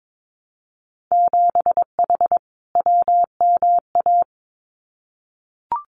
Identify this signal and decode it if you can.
Morse code, heard as a keyed tone, about 22 wpm, 700 Hz. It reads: 7H WMA
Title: Winter Field Day Exchanges Encore 22wpm